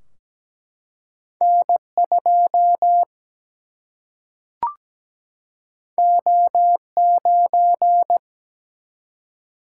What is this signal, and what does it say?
Morse code, heard as a keyed tone, about 17 wpm, 700 Hz. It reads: N2 O9
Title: Single-Letter Single-Number 17wpm